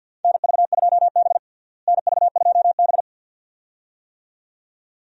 Morse code, heard as a keyed tone, about 52 wpm, 700 Hz. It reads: NV2B NV2B
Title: USA Call Signs - 4 Characters - Repeated 2x - 50wpm